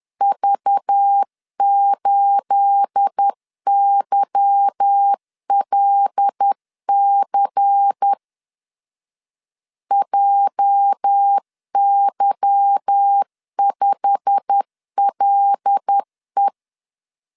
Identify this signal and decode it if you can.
Morse code, heard as a keyed tone, about 10 wpm, 800 Hz. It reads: V8YLC JY5LE
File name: LEZIONI / 9 286 10wpm 2735